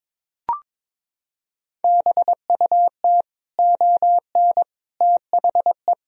Morse code, heard as a keyed tone, about 22 wpm, 700 Hz. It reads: BUT ON THE
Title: Sentences from Top 1000 Words - Review Part 4 - 22wpm